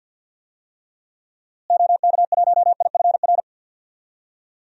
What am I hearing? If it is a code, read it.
Morse code, heard as a keyed tone, about 51 wpm, 700 Hz. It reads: OK1IFR
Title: DXCC-100 Call Signs - 6 Characters - Repeated 3x - 50wpm